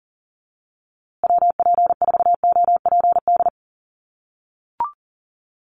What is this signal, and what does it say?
Morse code, heard as a keyed tone, about 40 wpm, 700 Hz. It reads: WP4OPD
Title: USA Call Signs Repeated 2x 40wpm